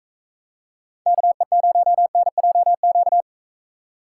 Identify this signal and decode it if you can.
Morse code, heard as a keyed tone, about 39 wpm, 700 Hz. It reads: KE0NJQ